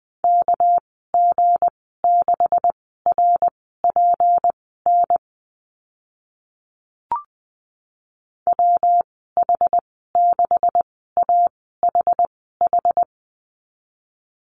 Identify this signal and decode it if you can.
Morse code, heard as a keyed tone, about 20 wpm, spaced as for 14 wpm, 700 Hz. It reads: KG6RPN WH6AHH
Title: 219 Lesson 35g - Number 0 - Call Signs 20wpm